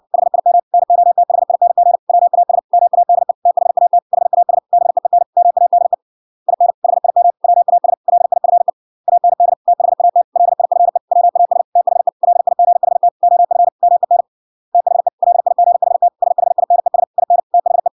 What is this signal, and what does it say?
Morse code, heard as a keyed tone, about 61 wpm, 700 Hz. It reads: HIM NOTHING WAS MADE THAT HAS BEEN MADE IN HIM WAS LIFE AND THAT LIFE WAS THE LIGHT OF MEN THE LIGHT SHINES IN THE